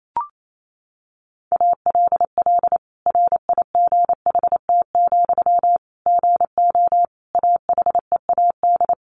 Morse code, heard as a keyed tone, about 28 wpm, 700 Hz. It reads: ALL RIGHT, GO AHEAD
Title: Sentences from Top 1100 Words 28wpm